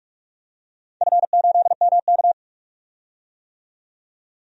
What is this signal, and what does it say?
Morse code, heard as a keyed tone, about 47 wpm, 700 Hz. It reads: F8MK
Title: DXCC-100 Call Signs - 4 Characters - 45wpm